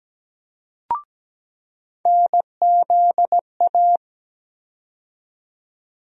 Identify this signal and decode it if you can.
Morse code, heard as a keyed tone, about 17 wpm, 700 Hz. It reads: NZA
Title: USA Call Sign Postfixes - 3 Letters 17wpm